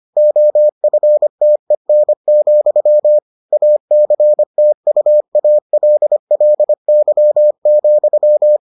Morse code, heard as a keyed tone, about 25 wpm, 600 Hz. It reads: OFTEN, ACTUALLY,